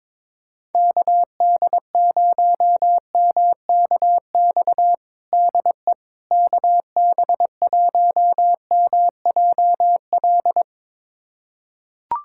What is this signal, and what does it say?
Morse code, heard as a keyed tone, about 22 wpm, 700 Hz. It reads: KD0MKX DE KB1MJL
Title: USA Callsign DE Callsign - 6 Characters - 22wpm